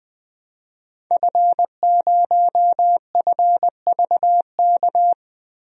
Morse code, heard as a keyed tone, about 20 wpm, 700 Hz. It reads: F0FVK
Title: DXCC-100 Call Signs - 5 Characters - Repeated 3x - 20wpm